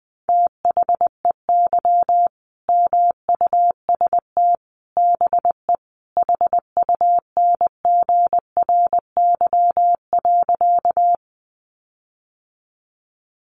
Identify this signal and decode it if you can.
Morse code, heard as a keyed tone, about 20 wpm, 700 Hz. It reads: THEY MUST BE HUNGRY.
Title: Sentences from Top 1400 Words 20wpm